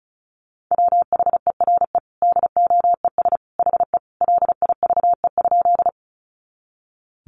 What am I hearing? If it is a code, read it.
Morse code, heard as a keyed tone, about 34 wpm, 700 Hz. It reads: WHERE DOES HE LIVE?
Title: Sentences from Top 200 Words 35wpm